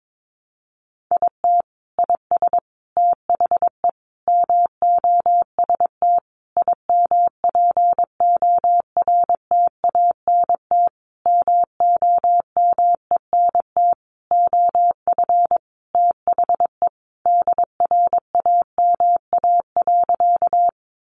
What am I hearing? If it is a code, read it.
Morse code, heard as a keyed tone, about 22 wpm, 700 Hz. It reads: IT IS THE MOST IMPORTANT MOMENT OF THE DRAMA.
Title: Sentences from Top 1500 Words 22wpm